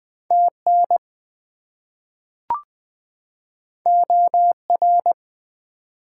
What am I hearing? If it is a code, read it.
Morse code, heard as a keyed tone, about 20 wpm, 700 Hz. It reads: TN OR
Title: US States and Canadian Provinces 20wpm